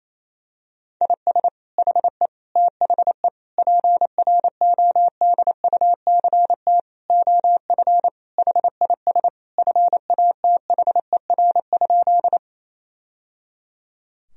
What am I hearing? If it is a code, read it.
Morse code, heard as a keyed tone, about 28 wpm, 700 Hz. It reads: IS HE THE PRODUCT OF HIS FATHER?
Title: Sentences from Top 1000 Words - Review Part 2 - 28wpm